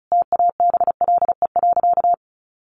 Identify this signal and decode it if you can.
Morse code, heard as a keyed tone, about 34 wpm, 700 Hz. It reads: TABLE.
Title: Sentences from Top 1500 Words 35wpm